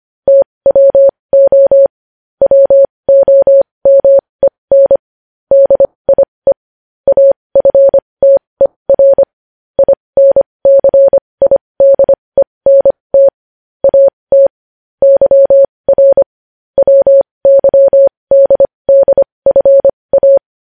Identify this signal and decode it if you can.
Morse code, heard as a keyed tone, about 25 wpm, spaced as for 20 wpm, 550 Hz. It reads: TWO WOMEN DIE AFTER INCIDENT AT YR WYDDFA
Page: News headlines in Morse code at 20 wpm